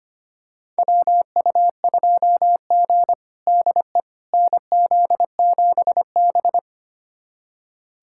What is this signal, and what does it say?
Morse code, heard as a keyed tone, about 25 wpm, 700 Hz. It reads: WU2G DE NZ7B
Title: USA Callsign DE Callsign - 4 Characters - 25wpm